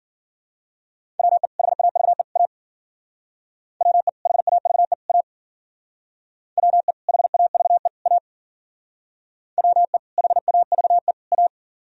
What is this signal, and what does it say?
Morse code, heard as a keyed tone, about 51 wpm, 700 Hz. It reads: WE HAVE A WE HAVE A WE HAVE A WE HAVE A